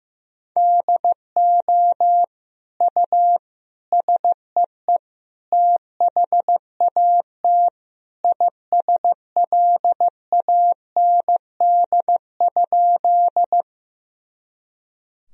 Morse code, heard as a keyed tone, about 15 wpm, 700 Hz. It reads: DO U SEE THAT ISLAND?
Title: Sentences from Top 500 Words and Abbreviations 15wpm